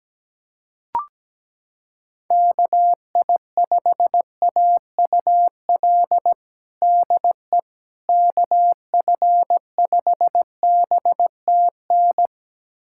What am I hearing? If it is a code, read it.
Morse code, heard as a keyed tone, about 17 wpm, 700 Hz. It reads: KI5AUL DE KF5BTN
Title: USA Callsign DE Callsign - 6 Characters - 17wpm